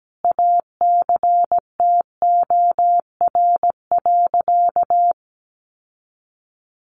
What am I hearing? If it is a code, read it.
Morse code, heard as a keyed tone, about 17 wpm, 700 Hz. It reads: ACTOR.
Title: Sentences from Top 1500 Words 17wpm